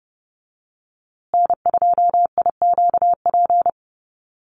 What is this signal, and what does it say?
Morse code, heard as a keyed tone, about 30 wpm, 700 Hz. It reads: N2IQP
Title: USA Call Signs Repeated 3x 30wpm